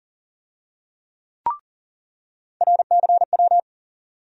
Morse code, heard as a keyed tone, about 40 wpm, 700 Hz. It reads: RCW
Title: USA Call Sign Postfixes - 3 Letters 40wpm